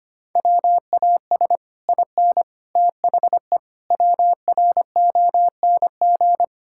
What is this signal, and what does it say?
Morse code, heard as a keyed tone, about 25 wpm, 700 Hz. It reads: WAS IN THE WRONG
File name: Sets of 9 Words - 25wpm